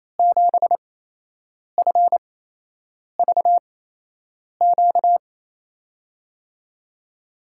Morse code, heard as a keyed tone, about 28 wpm, spaced as for 8 wpm, 700 Hz. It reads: 7FVQ
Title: USA Call Signs - 8x Character Spacing - 28wpm